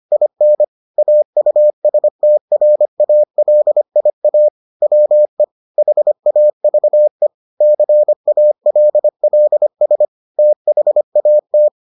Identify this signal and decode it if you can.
Morse code, heard as a keyed tone, about 25 wpm, 600 Hz. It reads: IN AUSTRALIA WE HAVE CALLS THAT